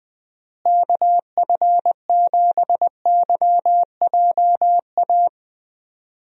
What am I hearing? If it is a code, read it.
Morse code, heard as a keyed tone, about 20 wpm, 700 Hz. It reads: KF7YJA
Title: USA Call Signs - 6 Characters - Repeated 3x - 20wpm